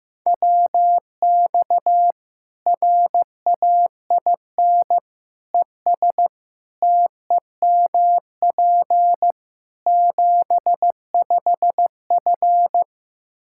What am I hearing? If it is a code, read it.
Morse code, heard as a keyed tone, about 15 wpm, 700 Hz. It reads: WX RAIN ES TEMP 75F
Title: Practice QSO - Exchange 2 - Info - 15wpm